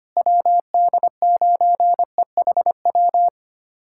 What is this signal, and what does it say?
Morse code, heard as a keyed tone, about 25 wpm, 700 Hz. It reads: WD9EHW
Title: USA Callsign DE Callsign 25wpm